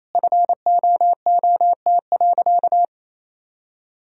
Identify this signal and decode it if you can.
Morse code, heard as a keyed tone, about 28 wpm, 700 Hz. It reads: FOOT.
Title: Sentences from Top 1000 Words - Review Part 3 - 28wpm